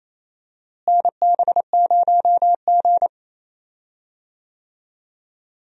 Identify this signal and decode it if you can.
Morse code, heard as a keyed tone, about 28 wpm, 700 Hz. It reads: NB0G
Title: USA Call Signs - 4 Characters - Repeated 3x - 28wpm